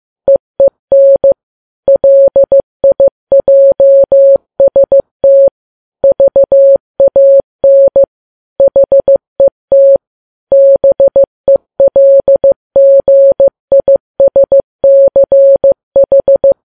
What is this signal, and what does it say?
Morse code, heard as a keyed tone, about 15 wpm, 550 Hz. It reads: EEN LIJST VAN HET BELGISCH
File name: Files Web tls / Morse on9fv / morse15kps